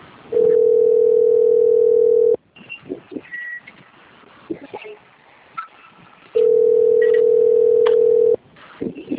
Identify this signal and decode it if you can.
Ringback tone (North America), 440 + 480 Hz.